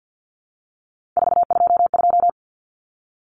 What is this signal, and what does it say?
Morse code, heard as a keyed tone, about 50 wpm, 700 Hz. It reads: 422
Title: Numbers - 3-Digit 50wpm